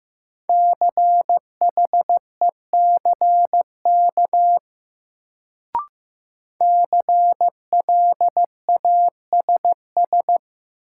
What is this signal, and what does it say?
Morse code, heard as a keyed tone, about 15 wpm, 700 Hz. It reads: CHECK CLASS
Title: Sentences from Top 1400 Words - New Word Drill Alt - 15wpm